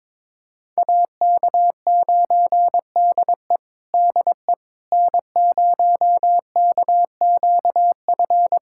Morse code, heard as a keyed tone, about 22 wpm, 700 Hz. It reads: AK9DE DE N0KQF
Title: USA Callsign DE Callsign - 5 Characters - 22wpm